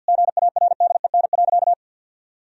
Morse code, heard as a keyed tone, about 50 wpm, 700 Hz. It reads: GARDEN.
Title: Sentences from Top 1000 Words - Review Part 4 - 50wpm